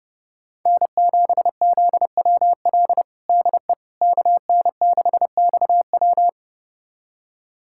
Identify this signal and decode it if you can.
Morse code, heard as a keyed tone, about 32 wpm, 700 Hz. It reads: N7ZWL DE KN6XW
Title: USA Callsign DE Callsign - 5 Characters - 30wpm